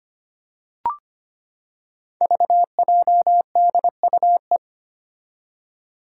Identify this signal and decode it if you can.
Morse code, heard as a keyed tone, about 25 wpm, 700 Hz. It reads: VJDUE
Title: Five Challenging Letter-Number Combinations 25wpm